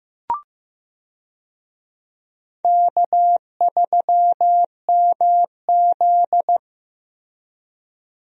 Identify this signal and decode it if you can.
Morse code, heard as a keyed tone, about 15 wpm, 700 Hz. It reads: K3MZ
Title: USA Call Signs - 4 Characters - Repeated 2x - 15wpm